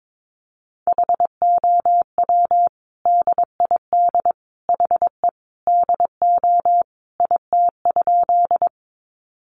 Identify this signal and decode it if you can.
Morse code, heard as a keyed tone, about 22 wpm, 700 Hz. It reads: HOW DID HE DO IT?
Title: Sentences from Top 100 Words 22wpm